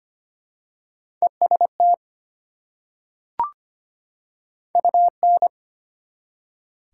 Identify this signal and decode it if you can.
Morse code, heard as a keyed tone, about 25 wpm, 700 Hz. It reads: EST UN
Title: Prefixes and Suffixes 25wpm